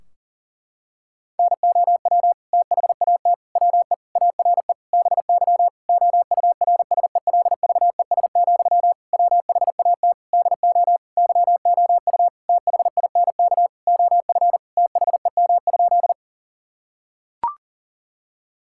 Morse code, heard as a keyed tone, about 40 wpm, 700 Hz. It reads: NOW THAT WE ARE BY OURSELVES, WHAT DO YOU THINK OF THEM?
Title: Sentences from Top 1600 Words 40wpm